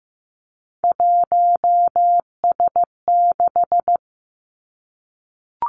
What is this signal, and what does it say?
Morse code, heard as a keyed tone, about 15 wpm, 700 Hz. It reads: 1S6
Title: Three Challenging Letter-Number Combinations 15wpm